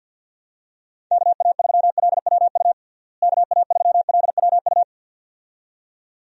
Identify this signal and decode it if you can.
Morse code, heard as a keyed tone, about 49 wpm, 700 Hz. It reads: KA3LWU KA3LWU
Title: USA Call Signs - 6 Characters - Repeated 2x - 50wpm